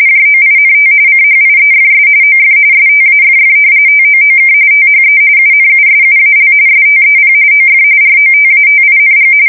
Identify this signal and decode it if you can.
RTTY, about 45.45 baud. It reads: JOINT CHIEFS OF STAFF TOOK THE UNUSUAL STEP OF ROTATING
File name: fsk170 / battleshipnj